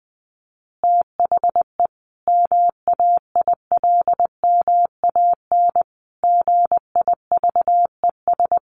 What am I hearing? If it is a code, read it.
Morse code, heard as a keyed tone, about 20 wpm, 700 Hz. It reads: THE MAILMAN GIVES
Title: Easy Phrases 20wpm